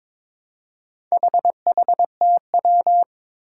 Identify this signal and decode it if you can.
Morse code, heard as a keyed tone, about 22 wpm, 700 Hz. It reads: HHTW
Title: Four Challenging Letter-Number Combinations 22wpm